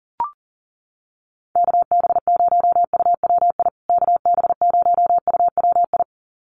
Morse code, heard as a keyed tone, about 40 wpm, 700 Hz. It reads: KB0UWI KB0UWI
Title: USA Call Signs Repeated 2x 40wpm